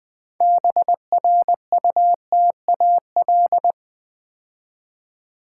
Morse code, heard as a keyed tone, about 20 wpm, 700 Hz. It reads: BRUTAL
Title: 6 Letter Words - First 3 Letters Given - 20wpm